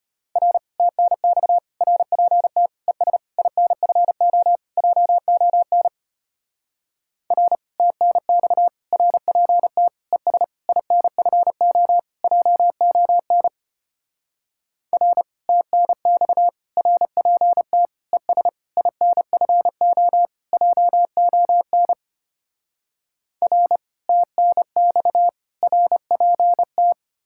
Morse code, heard as a keyed tone, about 33 wpm, 700 Hz. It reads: R TNX RPT ES INFO JON R TNX RPT ES INFO JON R TNX RPT ES INFO JON R TNX RPT